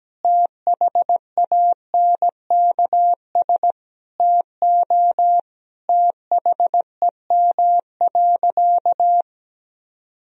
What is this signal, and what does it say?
Morse code, heard as a keyed tone, about 17 wpm, 700 Hz. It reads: THANKS TO THEM.
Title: Sentences from Top 1000 Words - Review Part 3 - 17wpm